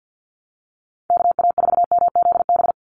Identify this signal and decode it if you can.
Morse code, heard as a keyed tone, about 50 wpm, 700 Hz. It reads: KA4MZB
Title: USA Call Signs 50wpm